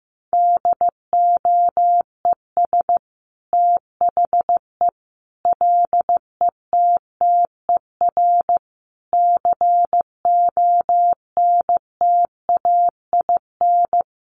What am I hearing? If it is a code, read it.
Morse code, heard as a keyed tone, about 15 wpm, 700 Hz. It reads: DOES THE LETTER CONTAIN
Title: Sentences from Top 1100 Words 15wpm